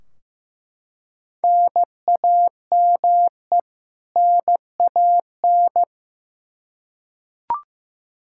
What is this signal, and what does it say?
Morse code, heard as a keyed tone, about 15 wpm, 700 Hz. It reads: NAME NAN
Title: Practice QSO - Exchange 1 - Beginning - 15wpm